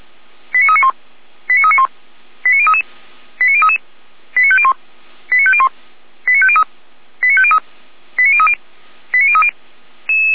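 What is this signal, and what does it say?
Five-tone selective call (ZVEI), heard as quick runs of five tones, tones of about 70 ms. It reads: 89391 89391 89033 89033 89501 89501 89503 89503 89030 89030
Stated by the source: ZVEI five-tone selective call 89050 received by SDR, from the feuerwehr-orlamuende.org fire brigade server.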